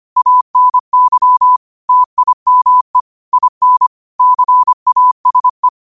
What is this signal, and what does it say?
Morse code, heard as a keyed tone, about 25 wpm, 1000 Hz. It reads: ANY TIME IN CASE